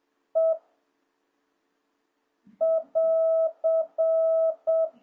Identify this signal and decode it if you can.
Morse code, heard as a keyed tone, about 7 wpm, 660 Hz. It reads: E <AR>